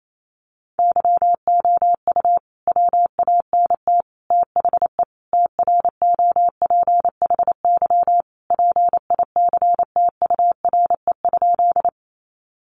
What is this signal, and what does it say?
Morse code, heard as a keyed tone, about 28 wpm, 700 Hz. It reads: YOU WANT THE TROPHY PICTURE?
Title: A Sound Of Thunder 28wpm